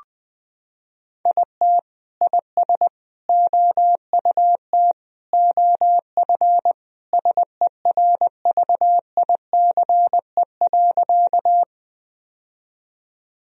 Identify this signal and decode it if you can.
Morse code, heard as a keyed tone, about 21 wpm, 700 Hz. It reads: IT IS OUT OF SERVICE.
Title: Sentences from Top 1000 Words - Review Part 4 - 20wpm